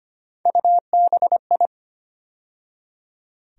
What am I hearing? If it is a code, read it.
Morse code, heard as a keyed tone, about 25 wpm, 700 Hz. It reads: UBI
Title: USA Call Sign Postfixes - 3 Letters 25wpm